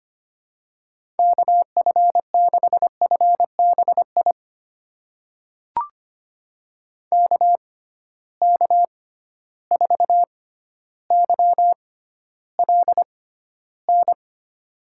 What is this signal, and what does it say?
Morse code, heard as a keyed tone, about 25 wpm, spaced as for 13 wpm, 700 Hz. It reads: KF6FBI KK4YLN
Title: USA Call Signs - 6 Characters - 6x Character Spacing - 25wpm